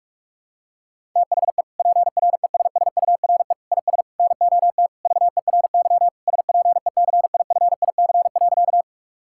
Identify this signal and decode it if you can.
Morse code, heard as a keyed tone, about 41 wpm, 700 Hz. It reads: THE PRESSURE IS NOT VERY SPECIFIC.
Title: Sentences from Top 1000 Words - Review Part 6 - 45wpm